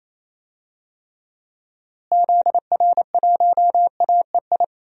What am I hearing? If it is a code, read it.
Morse code, heard as a keyed tone, about 28 wpm, 700 Hz. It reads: ZR1AEI
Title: DXCC-100 Call Signs - 6 Characters - Repeated 3x - 28wpm